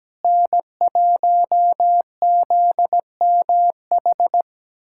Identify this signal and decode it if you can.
Morse code, heard as a keyed tone, about 17 wpm, 700 Hz. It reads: N1ZMH
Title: USA Callsign DE Callsign - 5 Characters - 17wpm